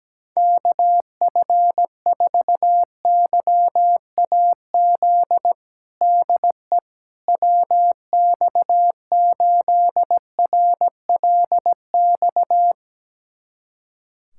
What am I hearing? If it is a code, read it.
Morse code, heard as a keyed tone, about 17 wpm, 700 Hz. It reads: KF4YAZ DE WX8RLX
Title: USA Callsign DE Callsign 17wpm